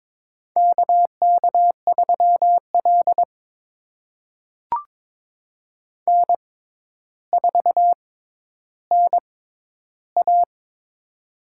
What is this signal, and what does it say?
Morse code, heard as a keyed tone, about 22 wpm, spaced as for 11 wpm, 700 Hz. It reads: KK3L N4NA
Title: USA Call Signs - 4 Characters - 6x Character Spacing - 22wpm